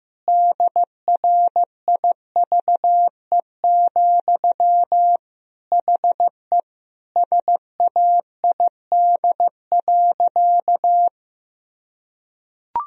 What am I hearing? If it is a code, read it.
Morse code, heard as a keyed tone, about 15 wpm, 700 Hz. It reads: DRIVE, HE SAID.
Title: Sentences from Top 1000 Words - Review Part 2 - 15wpm